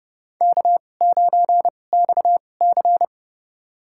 Morse code, heard as a keyed tone, about 30 wpm, spaced as for 22 wpm, 700 Hz. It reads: K9XC
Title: USA Call Signs - 4 Characters - 2x Character Spacing - 30wpm